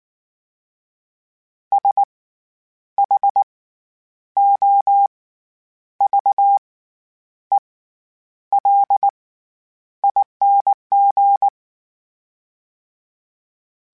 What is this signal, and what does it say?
Morse code, heard as a keyed tone, about 19 wpm, spaced as for 9 wpm, 800 Hz. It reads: SHOVELING
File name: Special INGs / Special INGs-mp3s / SlowToFaster ing @18